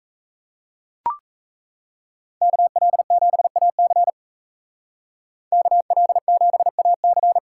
Morse code, heard as a keyed tone, about 36 wpm, 700 Hz. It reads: KL7AC KL7AC